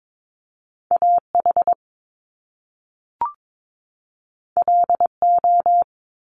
Morse code, heard as a keyed tone, about 22 wpm, 700 Hz. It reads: AH LO
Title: Top Two Letter Combinations in Words 22wpm